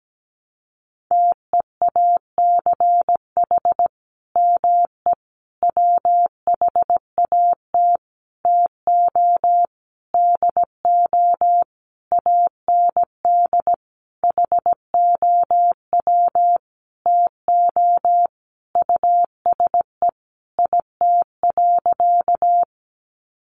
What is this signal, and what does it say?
Morse code, heard as a keyed tone, about 17 wpm, 700 Hz. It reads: TEACH ME WHAT TO DO AND HOW TO USE IT.
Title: Sentences from Top 500 Words 17wpm